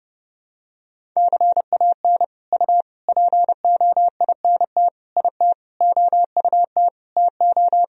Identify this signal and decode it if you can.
Morse code, heard as a keyed tone, about 30 wpm, 700 Hz. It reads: CAN U POINT IT OUT TO
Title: Sentences from Top 200 Words and Abbreviations 30wpm